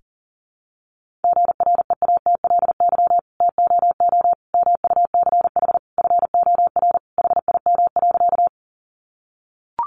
Morse code, heard as a keyed tone, about 37 wpm, 700 Hz. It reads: GREATLY TOO MUCH FOR HIM.
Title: Sentences from Top 1400 Words 40wpm